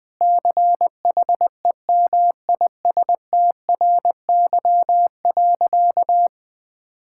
Morse code, heard as a keyed tone, about 20 wpm, 700 Hz. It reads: CHEMISTRY.
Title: Sentences from Top 1000 Words - Review Part 6 - 20wpm